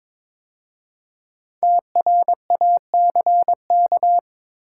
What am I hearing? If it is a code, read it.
Morse code, heard as a keyed tone, about 23 wpm, 700 Hz. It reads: TRACK
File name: 5 Letter Words - First 3 Letters Given - 22wpm